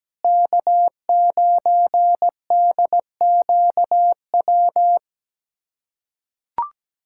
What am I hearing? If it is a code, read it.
Morse code, heard as a keyed tone, about 17 wpm, 700 Hz. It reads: K9DQW
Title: USA Callsign DE Callsign 17wpm